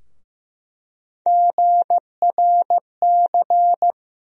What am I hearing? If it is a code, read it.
Morse code, heard as a keyed tone, about 16 wpm, 700 Hz. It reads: GRC
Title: USA Call Sign Postfixes - 3 Letters 15wpm